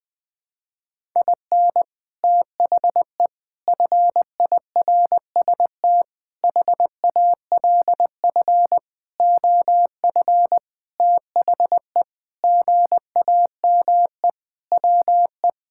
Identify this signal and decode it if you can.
Morse code, heard as a keyed tone, about 19 wpm, 700 Hz. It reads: IN THE FIRST HALF OF THE GAME WE
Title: Sentences with 6-Word Phrases - 20wpm